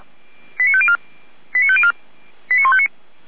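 Five-tone selective call (ZVEI), heard as quick runs of five tones, tones of about 70 ms. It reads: 89504 89504 89159
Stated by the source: ZVEI five-tone call 89078 received on SDR.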